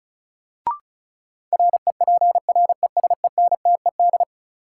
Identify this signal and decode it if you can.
Morse code, heard as a keyed tone, about 34 wpm, 700 Hz. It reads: REPRESENTED
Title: Sets of 5 Words - New Word Drill - 35wpm